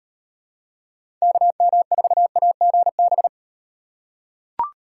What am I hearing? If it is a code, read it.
Morse code, heard as a keyed tone, about 38 wpm, 700 Hz. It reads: KM4AGB